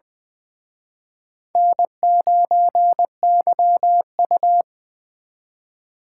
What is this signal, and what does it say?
Morse code, heard as a keyed tone, about 24 wpm, 700 Hz. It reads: N9YU
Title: Speed Racing - USA Call Signs 20wpm